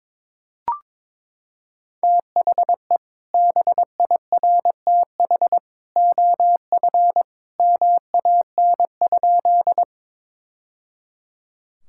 Morse code, heard as a keyed tone, about 21 wpm, 700 Hz. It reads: THE BIRTH OF MAN?
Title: Sentences from Top 1000 Words - Review Part 5 - 22wpm